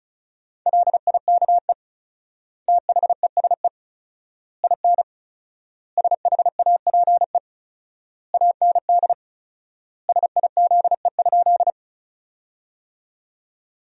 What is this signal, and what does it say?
Morse code, heard as a keyed tone, about 36 wpm, 700 Hz. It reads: LIKE THESE IN SHAPE AND SIZE?
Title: 4x Word-Spacing - Sentences from Top 500 Words - 35wpm